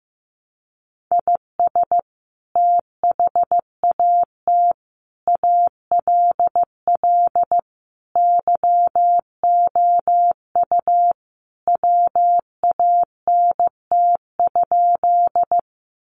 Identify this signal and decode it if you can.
Morse code, heard as a keyed tone, about 15 wpm, 700 Hz. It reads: IS THAT ALL YOU WANT?
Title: Sentences from Top 200 Words 15wpm